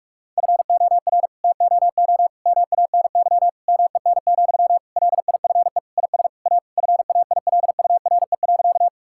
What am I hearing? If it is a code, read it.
Morse code, heard as a keyed tone, about 46 wpm, 700 Hz. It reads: FOR TOO MANY MEN, LIFE IS A FAILURE.
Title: Sentences from Top 1000 Words - Review Part 6 - 45wpm